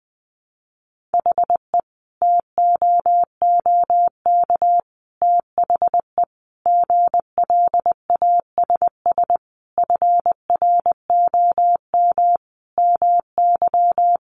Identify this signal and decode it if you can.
Morse code, heard as a keyed tone, about 20 wpm, 700 Hz. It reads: HE TOOK THE GLASS FROM MY
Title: Sentences from Top 1300 Words 20wpm